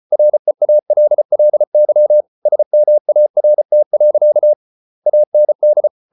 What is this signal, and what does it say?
Morse code, heard as a keyed tone, about 34 wpm, 600 Hz. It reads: REALLY SMART. AND